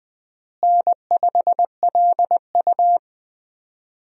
Morse code, heard as a keyed tone, about 20 wpm, 700 Hz. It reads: N5LU
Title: USA Call Signs - 4 Characters - Repeated 3x - 20wpm